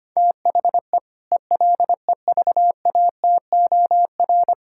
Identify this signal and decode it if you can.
Morse code, heard as a keyed tone, about 25 wpm, 700 Hz. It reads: THE ELEVATOR
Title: Sentences from Top 1600 Words 25wpm